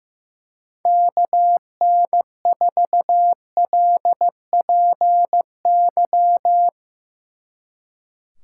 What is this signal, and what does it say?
Morse code, heard as a keyed tone, about 16 wpm, 700 Hz. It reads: KN4LPY
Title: Speed Racing - USA Call Signs 15wpm